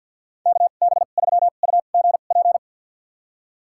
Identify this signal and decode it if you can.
Morse code, heard as a keyed tone, about 52 wpm, spaced as for 36 wpm, 700 Hz. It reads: KB3UGP
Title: USA Call Signs - 2x Character Spacing - 50wpm